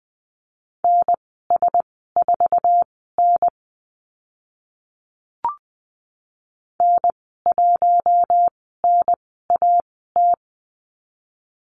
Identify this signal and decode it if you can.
Morse code, heard as a keyed tone, about 20 wpm, spaced as for 14 wpm, 700 Hz. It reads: NS4N N1NAT
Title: 048 Lesson 9g - Number 4 - Call Signs 20wpm